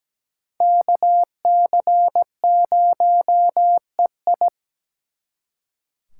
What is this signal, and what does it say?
Morse code, heard as a keyed tone, about 20 wpm, 700 Hz. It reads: KC0EI
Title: Speed Racing - USA 2x2 Call Signs 17wpm